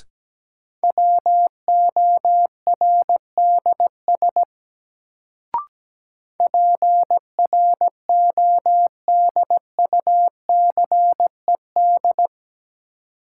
Morse code, heard as a keyed tone, about 17 wpm, 700 Hz. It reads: WORDS PRODUCED
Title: Sentences from Top 1000 Words - Review Part 6 - New Word Drill - 17wpm